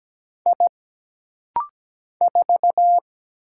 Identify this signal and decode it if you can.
Morse code, heard as a keyed tone, about 17 wpm, 700 Hz. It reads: I 4
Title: Single Letter-Number - Rapid-Fire 17wpm